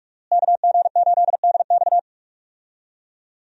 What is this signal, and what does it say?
Morse code, heard as a keyed tone, about 42 wpm, 700 Hz. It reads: KG8DX
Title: USA Call Signs - 5 Characters - Repeated 2x - 45wpm